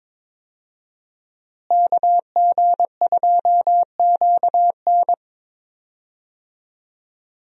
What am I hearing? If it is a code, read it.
Morse code, heard as a keyed tone, about 22 wpm, 700 Hz. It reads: KG2QN
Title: USA Call Signs - 5 Characters - Repeated 2x - 22wpm